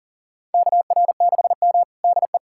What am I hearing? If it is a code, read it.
Morse code, heard as a keyed tone, about 37 wpm, 700 Hz. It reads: KR6M DE
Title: USA Callsign DE Callsign - 4 Characters - 40wpm